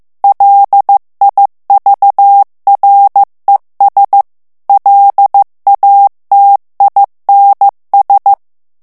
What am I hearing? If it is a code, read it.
Morse code, heard as a keyed tone, about 15 wpm, 800 Hz. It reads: LIVRES LATINS
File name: morse mp3 / FRANCAIS TEXTE CLAIR 2 VITESSE 14 MOTS MINUTES